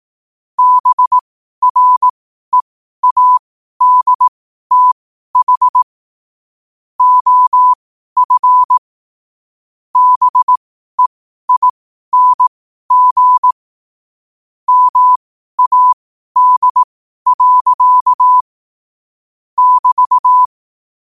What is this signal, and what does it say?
Morse code, heard as a keyed tone, about 18 wpm, spaced as for 12 wpm, 1000 Hz. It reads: BREADTH OF BEING MAD. <BT>
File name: qotdfiles / 12wpm